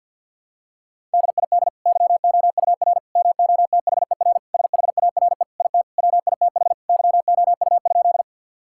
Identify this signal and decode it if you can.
Morse code, heard as a keyed tone, about 46 wpm, 700 Hz. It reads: DID YOUR MOTHER SHARE IT WITH YOU?
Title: Sentences from Top 1000 Words - Review Part 6 - 50wpm